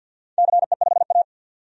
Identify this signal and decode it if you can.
Morse code, heard as a keyed tone, about 48 wpm, 700 Hz. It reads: CE5A
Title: DXCC-100 Call Signs - 4 Characters - 50wpm